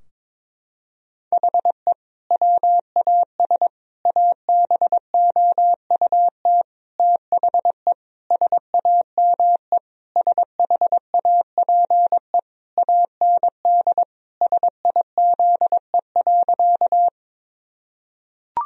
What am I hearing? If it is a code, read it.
Morse code, heard as a keyed tone, about 22 wpm, 700 Hz. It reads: HE WAS ABOUT THE SAME SHAPE AND SIZE.
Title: Sentences from Top 1000 Words - Review Part 3 - 22wpm